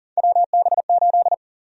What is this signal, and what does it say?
Morse code, heard as a keyed tone, about 39 wpm, 700 Hz. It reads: WB8
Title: USA Call Sign Prefixes - Repeated 2x - 40wpm